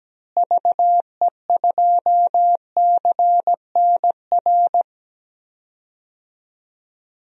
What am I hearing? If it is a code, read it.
Morse code, heard as a keyed tone, about 17 wpm, 700 Hz. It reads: VE2CNR
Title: DXCC-100 Call Signs - 6 Characters - Repeated 3x - 17wpm